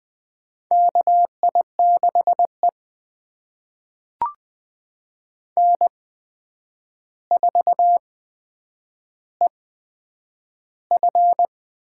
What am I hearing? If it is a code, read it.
Morse code, heard as a keyed tone, about 20 wpm, spaced as for 9 wpm, 700 Hz. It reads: KI6E N4EF
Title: USA Call Signs - 4 Characters - 8x Character Spacing - 20wpm